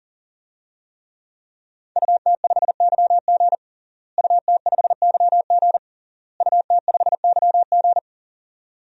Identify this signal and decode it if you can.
Morse code, heard as a keyed tone, about 40 wpm, 700 Hz. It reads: UT5YG UT5YG UT5YG
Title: DXCC-100 Call Signs - 5 Characters - Repeated 3x - 40wpm